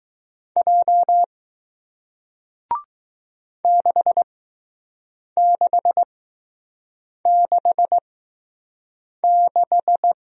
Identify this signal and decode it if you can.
Morse code, heard as a keyed tone, about 20 wpm, spaced as for 6 wpm, 700 Hz. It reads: J 6666